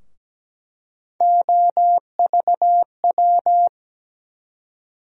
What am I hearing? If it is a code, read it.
Morse code, heard as a keyed tone, about 17 wpm, 700 Hz. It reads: OVW
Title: USA Call Sign Postfixes - 3 Letters 17wpm